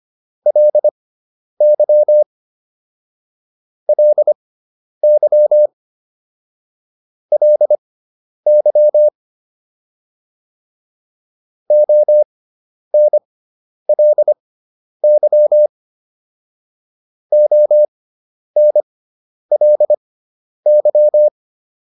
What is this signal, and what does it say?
Morse code, heard as a keyed tone, about 25 wpm, spaced as for 10 wpm, 600 Hz. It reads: LY LY LY ONLY ONLY